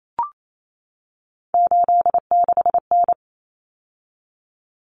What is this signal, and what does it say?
Morse code, heard as a keyed tone, about 28 wpm, 700 Hz. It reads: 86N
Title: Common Cut Numbers - 3-Digit 28wpm